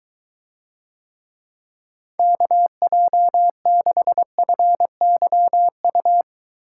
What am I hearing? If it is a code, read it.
Morse code, heard as a keyed tone, about 23 wpm, 700 Hz. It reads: KJ6FYU